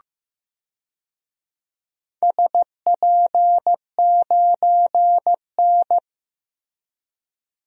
Morse code, heard as a keyed tone, about 15 wpm, 700 Hz. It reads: SP9N
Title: DXCC-100 Call Signs - 4 Characters - Repeated 2x - 15wpm